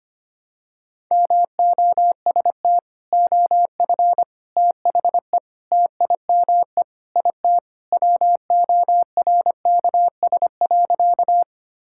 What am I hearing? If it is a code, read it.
Morse code, heard as a keyed tone, about 25 wpm, 700 Hz. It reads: MOST OF THE TIME IT WORKS.
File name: practice / 25wpm / Sentences with 4-Word Phrases - 25wpm